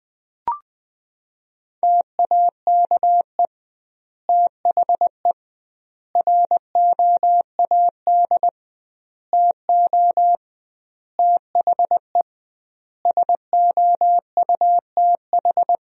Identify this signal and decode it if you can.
Morse code, heard as a keyed tone, about 19 wpm, 700 Hz. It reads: TAKE THE ROAD TO THE SOUTH
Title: 2x Word-Spacing - Sentences from Top 400 Words - 20wpm